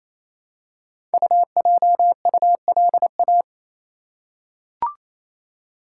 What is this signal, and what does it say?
Morse code, heard as a keyed tone, about 29 wpm, 700 Hz. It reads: UJULA
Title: Code Groups - 5 Letters - 28wpm